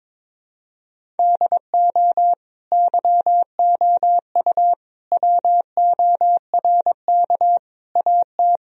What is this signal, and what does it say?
Morse code, heard as a keyed tone, about 22 wpm, 700 Hz. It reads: DO YOU WORK AT
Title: Sentences from Top 1000 Words - Review Part 1 - 22wpm